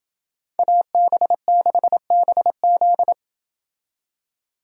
Morse code, heard as a keyed tone, about 28 wpm, 700 Hz. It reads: AB6BZ